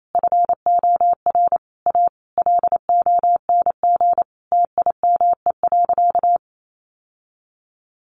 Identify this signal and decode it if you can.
Morse code, heard as a keyed tone, about 28 wpm, 700 Hz. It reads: FOR A LONG TIME.
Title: Sentences from Top 1000 Words 28wpm